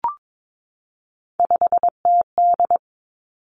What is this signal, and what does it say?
Morse code, heard as a keyed tone, about 22 wpm, 700 Hz. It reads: 5TD
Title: Three Challenging Letter-Number Combinations 22wpm